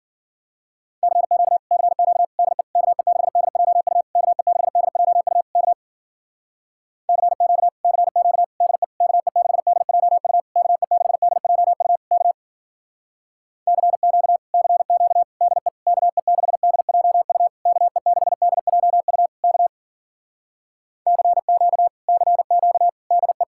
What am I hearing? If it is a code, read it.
Morse code, heard as a keyed tone, about 52 wpm, 700 Hz. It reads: CQ CQ DE KE6DJU KE6DJU K CQ CQ DE KE6DJU KE6DJU K CQ CQ DE KE6DJU KE6DJU K CQ CQ DE